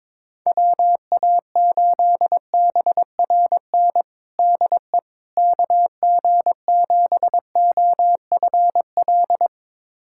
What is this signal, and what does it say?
Morse code, heard as a keyed tone, about 23 wpm, 700 Hz. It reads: WA8BRN DE KG7OFL
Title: USA Callsign DE Callsign - 6 Characters - 22wpm